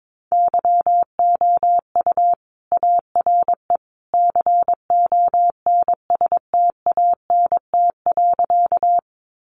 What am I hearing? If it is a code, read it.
Morse code, heard as a keyed tone, about 22 wpm, 700 Hz. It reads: YOU ARE CONSTANT.
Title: Sentences from Top 1500 Words 22wpm